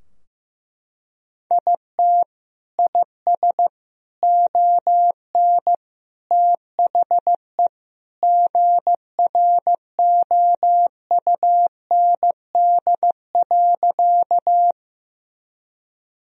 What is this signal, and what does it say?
Morse code, heard as a keyed tone, about 15 wpm, 700 Hz. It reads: IT IS ON THE GROUND.
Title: Sentences from Top 1000 Words - Review Part 2 - 15wpm